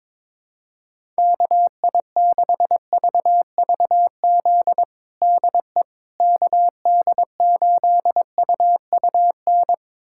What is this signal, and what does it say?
Morse code, heard as a keyed tone, about 22 wpm, 700 Hz. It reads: KI6VVZ DE KD8UUN
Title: USA Callsign DE Callsign - 6 Characters - 22wpm